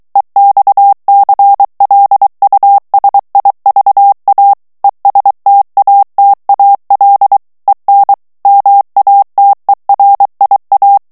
Morse code, heard as a keyed tone, about 23 wpm, 800 Hz. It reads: EXCLUSIVA ESTATAL EN MATERIA
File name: morse mp3 / ESPAGNOL TEXTE EN CLAIR 2 VITESSE 22 MOTS MINUTES